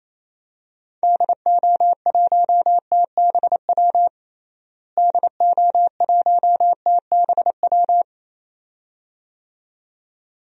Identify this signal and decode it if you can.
Morse code, heard as a keyed tone, about 28 wpm, 700 Hz. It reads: DO1TBW DO1TBW
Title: DXCC-100 Call Signs - 6 Characters - Repeated 2x - 28wpm